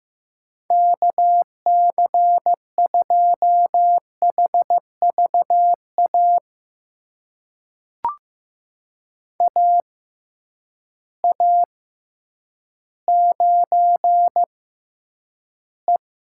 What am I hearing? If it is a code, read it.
Morse code, heard as a keyed tone, about 15 wpm, spaced as for 9 wpm, 700 Hz. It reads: KC2HVA AA9E
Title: USA Call Signs - 6x Character Spacing - 15wpm